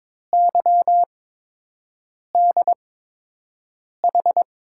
Morse code, heard as a keyed tone, about 22 wpm, spaced as for 6 wpm, 700 Hz. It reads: YDH
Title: USA Call Signs - 6 Characters - 8x Character Spacing - 22wpm